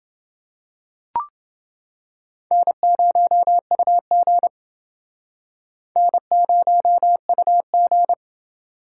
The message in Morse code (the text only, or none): N0UG N0UG